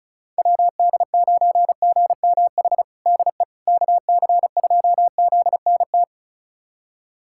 Morse code, heard as a keyed tone, about 36 wpm, 700 Hz. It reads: WD9GMH DE KC2ZNT
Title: USA Callsign DE Callsign 35wpm